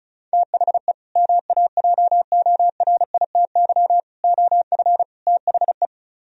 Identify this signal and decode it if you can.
Morse code, heard as a keyed tone, about 43 wpm, 700 Hz. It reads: THE MAJORITY OF THE
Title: Speed-Racing - Sets of 4 Words 35wpm